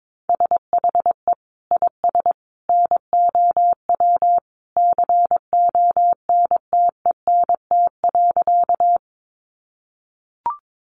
Morse code, heard as a keyed tone, about 22 wpm, 700 Hz. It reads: SHE IS NOW CONTENT.
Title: Sentences from Top 900 Words 22wpm